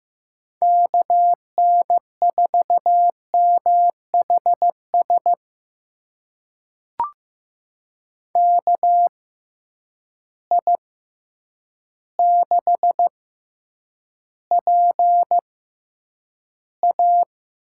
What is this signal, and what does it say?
Morse code, heard as a keyed tone, about 15 wpm, spaced as for 8 wpm, 700 Hz. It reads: KN4MHS KI6PA
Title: USA Call Signs - 6 Characters - 6x Character Spacing - 15wpm